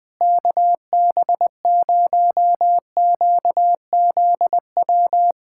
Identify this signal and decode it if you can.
Morse code, heard as a keyed tone, about 20 wpm, 700 Hz. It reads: KB0QZW
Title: USA Callsign DE Callsign 20wpm